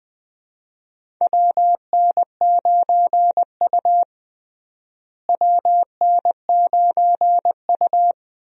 Morse code, heard as a keyed tone, about 20 wpm, 700 Hz. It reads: WN9U WN9U
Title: USA Call Signs - 4 Characters - Repeated 2x - 20wpm